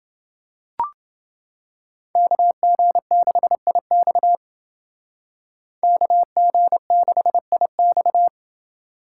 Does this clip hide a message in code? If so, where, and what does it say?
Yes: Morse code from 2.2 to 8.3 s, heard as a keyed tone, about 28 wpm, 700 Hz. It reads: KG6IX KG6IX